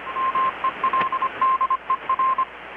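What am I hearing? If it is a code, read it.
Morse code, heard as a keyed tone, about 25 wpm, 1050 Hz, with radio noise. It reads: MELDER
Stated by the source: Morse from KPH's Night of Nights XX maritime radiotelegraphy event, received in Chiapas, Mexico about 3,800 km away on a Kenwood R-600 with a grounded 10 m long-wire antenna.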